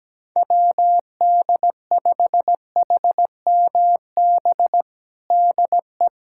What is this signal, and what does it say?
Morse code, heard as a keyed tone, about 17 wpm, 700 Hz. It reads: WD5HMB DE
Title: USA Callsign DE Callsign 17wpm